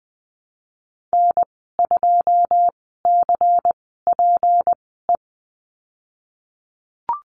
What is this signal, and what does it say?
Morse code, heard as a keyed tone, about 20 wpm, spaced as for 15 wpm, 700 Hz. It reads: N2CPE
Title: 161 Lesson 26g - Letter g - Call Signs 20wpm